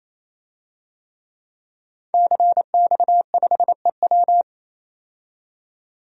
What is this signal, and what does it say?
Morse code, heard as a keyed tone, about 28 wpm, 700 Hz. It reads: CX5EW
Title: DXCC-100 Call Signs - 5 Characters - 28wpm